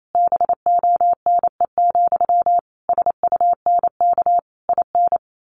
Morse code, heard as a keyed tone, about 28 wpm, 700 Hz. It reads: BONE, SUNK IN
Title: A Sound Of Thunder 28wpm